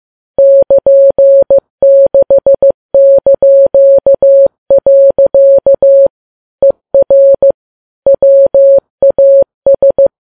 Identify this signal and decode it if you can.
Morse code, heard as a keyed tone, about 15 wpm, 550 Hz. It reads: <KN>6). ER WAS
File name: Files Web tls / Morse on9fv / morse15kps